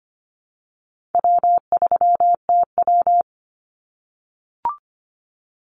Morse code, heard as a keyed tone, about 25 wpm, 700 Hz. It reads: W3TW
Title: Call Signs - 25wpm